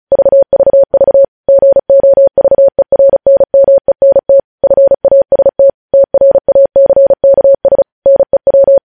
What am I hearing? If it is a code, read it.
Morse code, heard as a keyed tone, about 36 wpm, 550 Hz. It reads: VVV GOVERNMENT FAST TRACKS NEW
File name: podcast / News headlines-1x-35x35